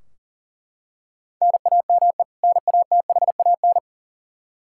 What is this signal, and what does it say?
Morse code, heard as a keyed tone, about 40 wpm, 700 Hz. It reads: NAME NATHAN
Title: Practice QSO - Exchange 1 - Beginning - 40wpm